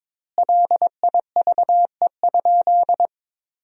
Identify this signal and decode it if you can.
Morse code, heard as a keyed tone, about 22 wpm, 700 Hz. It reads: LIVE?
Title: 2x Word-Spacing - Sentences from Top 200 Words - 22wpm